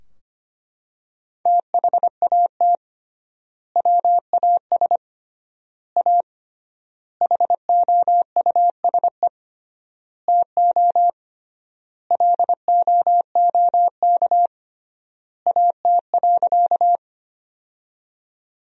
Morse code, heard as a keyed tone, about 25 wpm, 700 Hz. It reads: THAT WAS A HOUSE TO LOOK AT.
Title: 3x Word-Spacing - Sentences from Top 200 Words - 25wpm